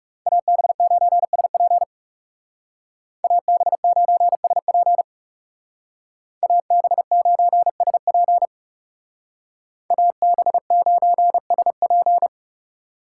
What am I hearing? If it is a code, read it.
Morse code, heard as a keyed tone, about 34 wpm, 700 Hz. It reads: AB9SP AB9SP AB9SP AB9SP
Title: Speed Racing - USA 2x2 Call Signs 30wpm